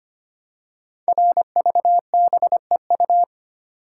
Morse code, heard as a keyed tone, about 25 wpm, 700 Hz. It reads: RVBEU
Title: Code Groups - 5 Letters - 25wpm